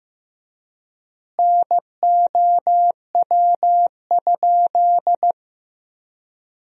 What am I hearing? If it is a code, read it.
Morse code, heard as a keyed tone, about 15 wpm, 700 Hz. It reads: NOW?
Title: 3x Word-Spacing - Sentences from Top 100 Words - 15wpm